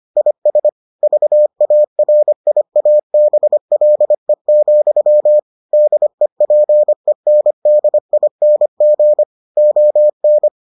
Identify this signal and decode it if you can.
Morse code, heard as a keyed tone, about 25 wpm, 600 Hz. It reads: IS VARIABLE, DEPENDING ON